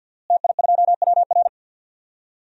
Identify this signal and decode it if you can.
Morse code, heard as a keyed tone, about 48 wpm, 700 Hz. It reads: TI2WR
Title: DXCC-100 Call Signs - 5 Characters - Repeated 3x - 50wpm